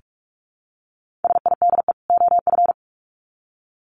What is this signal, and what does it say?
Morse code, heard as a keyed tone, about 44 wpm, 700 Hz. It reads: SIDE OF
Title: Sets of 2 Words 45wpm